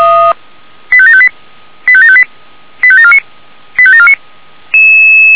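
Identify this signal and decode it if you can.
Five-tone selective call (ZVEI), heard as quick runs of five tones, tones of about 70 ms. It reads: 85758 85758 85749 85749
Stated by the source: ZVEI five-tone selective call received by SDR.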